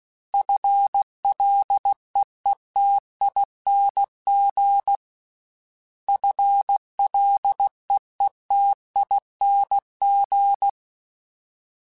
Morse code, heard as a keyed tone, about 16 wpm, 800 Hz. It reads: FLEETING FLEETING
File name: FL / FL mp3s / FL Prefixes 3x2 Twice @15